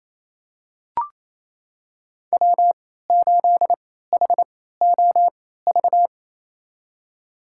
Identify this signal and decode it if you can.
Morse code, heard as a keyed tone, about 28 wpm, spaced as for 16 wpm, 700 Hz. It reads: W8HOV
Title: USA Call Signs - 3x Character Spacing - 28wpm